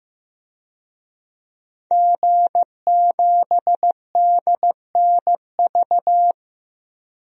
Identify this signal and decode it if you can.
Morse code, heard as a keyed tone, about 15 wpm, 700 Hz. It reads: G7DNV
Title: DXCC-100 Call Signs - 5 Characters - Repeated 3x - 15wpm